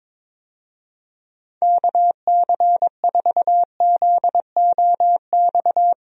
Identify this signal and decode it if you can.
Morse code, heard as a keyed tone, about 21 wpm, 700 Hz. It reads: KC4ZOX
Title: USA Call Signs - 6 Characters - 22wpm